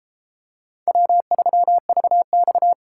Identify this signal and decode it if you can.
Morse code, heard as a keyed tone, about 33 wpm, 700 Hz. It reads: W3VX